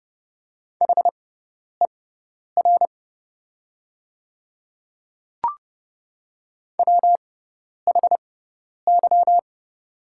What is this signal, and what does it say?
Morse code, heard as a keyed tone, about 30 wpm, spaced as for 10 wpm, 700 Hz. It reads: HER WHY
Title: 6x Character Spacing - Top 500 Words - 30wpm